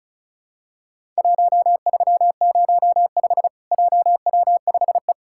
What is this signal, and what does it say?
Morse code, heard as a keyed tone, about 34 wpm, 700 Hz. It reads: 1305 JW5E
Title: Worked All Europe DX Contest - QTC Exchange - 35wpm